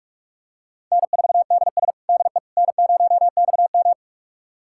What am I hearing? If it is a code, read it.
Morse code, heard as a keyed tone, about 46 wpm, 700 Hz. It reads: N4DS DE N0XM
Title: USA Callsign DE Callsign - 4 Characters - 45wpm